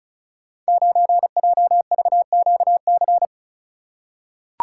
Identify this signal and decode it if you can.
Morse code, heard as a keyed tone, about 36 wpm, 700 Hz. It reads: 9JVQC
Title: Code Groups - 5 Letter:Numbers - 35wpm